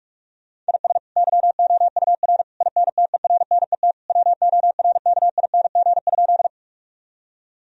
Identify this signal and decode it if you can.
Morse code, heard as a keyed tone, about 46 wpm, 700 Hz. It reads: IS YOUR INTERNET WORKING?
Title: Sentences from Top 1000 Words - Review Part 4 - 45wpm